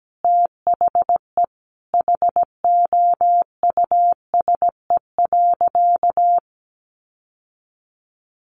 Morse code, heard as a keyed tone, about 17 wpm, 700 Hz. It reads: THE HOUSE.
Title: Sentences from Top 1400 Words 17wpm